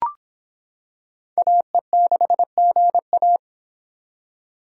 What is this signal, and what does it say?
Morse code, heard as a keyed tone, about 24 wpm, 700 Hz. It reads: AE6GA